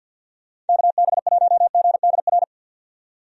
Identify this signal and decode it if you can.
Morse code, heard as a keyed tone, about 50 wpm, 700 Hz. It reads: KB1GDR
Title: USA Call Signs - 6 Characters - 50wpm